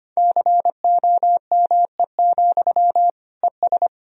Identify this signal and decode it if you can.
Morse code, heard as a keyed tone, about 25 wpm, 700 Hz. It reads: COME, ES
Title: Sentences from Top 500 Words and Abbreviations 25wpm